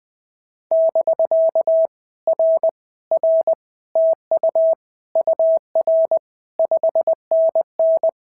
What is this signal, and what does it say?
Morse code, heard as a keyed tone, about 19 wpm, 650 Hz. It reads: <BK> R R TU UR 5NN
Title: POTA QSO Practice - 20wpm